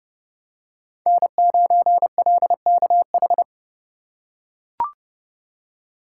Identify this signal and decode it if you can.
Morse code, heard as a keyed tone, about 30 wpm, 700 Hz. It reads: N9LKH
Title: USA Call Signs - 5 Characters - 3x Character Spacing - 30wpm